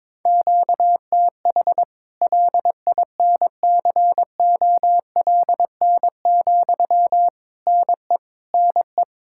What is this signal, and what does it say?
Morse code, heard as a keyed tone, about 22 wpm, 700 Hz. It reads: QTH LINCOLN, NE NE
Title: Practice QSO - Exchange 1 - Beginning - 22wpm